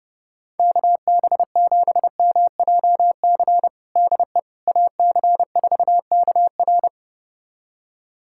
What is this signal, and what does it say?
Morse code, heard as a keyed tone, about 32 wpm, 700 Hz. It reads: KB7MJC DE AC4KR
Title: USA Callsign DE Callsign 30wpm